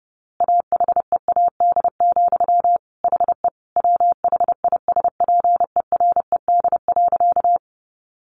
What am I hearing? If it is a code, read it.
Morse code, heard as a keyed tone, about 28 wpm, 700 Hz. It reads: AHEAD, HE WHISPERED.
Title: A Sound Of Thunder 30wpm